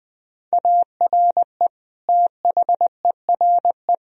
Morse code, heard as a keyed tone, about 20 wpm, 700 Hz. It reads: ARE THERE
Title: Sentences from Top 1000 Words - Review Part 4 - 20wpm